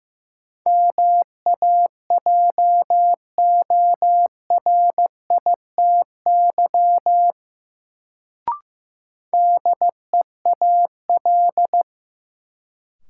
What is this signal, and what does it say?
Morse code, heard as a keyed tone, about 15 wpm, 700 Hz. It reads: MAJORITY DEAL
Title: Sets of 3 Words - New Word Drill - 15wpm